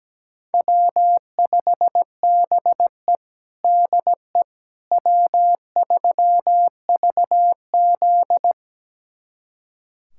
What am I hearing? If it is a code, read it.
Morse code, heard as a keyed tone, about 17 wpm, 700 Hz. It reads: W5BE DE W3VZ
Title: USA Callsign DE Callsign - 4 Characters - 17wpm